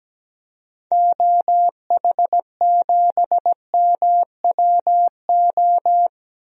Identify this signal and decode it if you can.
Morse code, heard as a keyed tone, about 17 wpm, 700 Hz. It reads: OH7MWO
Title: DXCC-100 Call Signs - 6 Characters - 17wpm